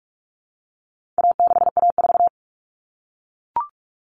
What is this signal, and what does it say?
Morse code, heard as a keyed tone, about 44 wpm, 700 Hz. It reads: A6A4
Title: All Cut Numbers - 4-Digit 45wpm